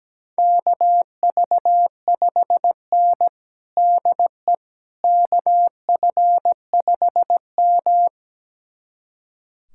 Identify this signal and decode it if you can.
Morse code, heard as a keyed tone, about 17 wpm, 700 Hz. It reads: KV5N DE KF5M
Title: USA Callsign DE Callsign - 4 Characters - 17wpm